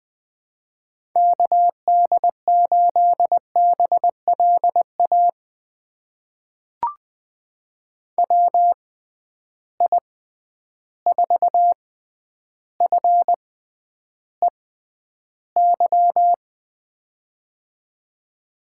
Morse code, heard as a keyed tone, about 20 wpm, spaced as for 10 wpm, 700 Hz. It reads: KD8BLA WI4FEY
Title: USA Call Signs - 6 Characters - 6x Character Spacing - 20wpm